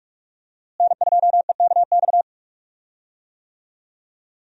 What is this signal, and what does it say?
Morse code, heard as a keyed tone, about 44 wpm, 700 Hz. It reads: N2EKX
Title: USA Call Signs - 5 Characters - Repeated 2x - 45wpm